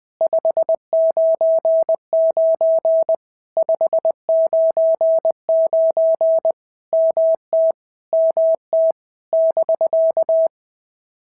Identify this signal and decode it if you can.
Morse code, heard as a keyed tone, about 21 wpm, 650 Hz. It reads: 599 599 MT MT <BK>
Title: POTA QSO Practice - 20wpm